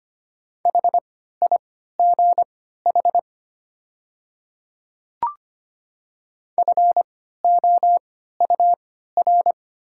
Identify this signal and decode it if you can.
Morse code, heard as a keyed tone, about 25 wpm, spaced as for 14 wpm, 700 Hz. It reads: HIGH FOUR
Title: 3x Character Spacing - Top 500 Words - 25wpm